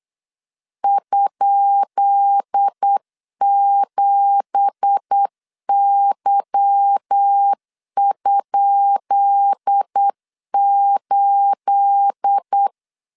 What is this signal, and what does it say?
Morse code, heard as a keyed tone, about 8 wpm, 800 Hz. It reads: ?7Y?8
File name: LEZIONI / 10 SIGNS / 10D COMA 8wpm 3749